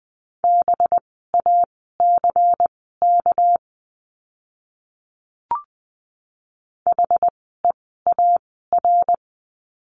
Morse code, heard as a keyed tone, about 20 wpm, spaced as for 15 wpm, 700 Hz. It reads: BACK HEAR
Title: 206 Lesson 33f - Letter j - Words 20wpm